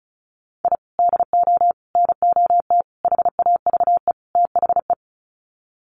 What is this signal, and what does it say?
Morse code, heard as a keyed tone, about 34 wpm, 700 Hz. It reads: I DO NOT HAVE THE
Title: Sets of 5 Words 35wpm